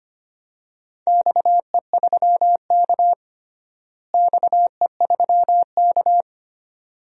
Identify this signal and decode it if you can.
Morse code, heard as a keyed tone, about 25 wpm, 700 Hz. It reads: XE3K XE3K
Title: DXCC-100 Call Signs - 4 Characters - Repeated 2x - 25wpm